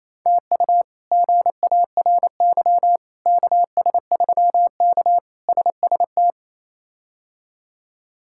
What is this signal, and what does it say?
Morse code, heard as a keyed tone, about 28 wpm, 700 Hz. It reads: TU GARY KS3K SST
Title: K1USN SST QSO Practice - 28wpm